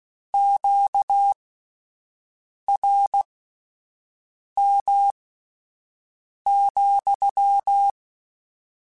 Morse code, heard as a keyed tone, about 16 wpm, spaced as for 5 wpm, 800 Hz. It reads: QRM,
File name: Q Calls / Q Calls-mp3s / Q Calls sppc @10